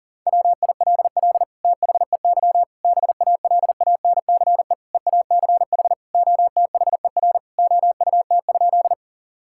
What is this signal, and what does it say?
Morse code, heard as a keyed tone, about 40 wpm, 690 Hz. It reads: WILL THEY BALANCE EACH OTHER OUT?
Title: Sentences from Top 1000 Words - Review Part 6 - 40wpm